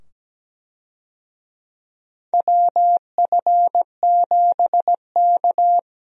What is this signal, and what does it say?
Morse code, heard as a keyed tone, about 17 wpm, 700 Hz. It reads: WF7K
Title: USA Call Signs - 4 Characters - 17wpm